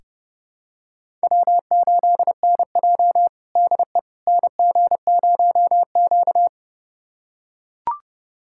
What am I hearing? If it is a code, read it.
Morse code, heard as a keyed tone, about 32 wpm, 700 Hz. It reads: W8NJ DE NG0Q
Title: USA Callsign DE Callsign - 4 Characters - 30wpm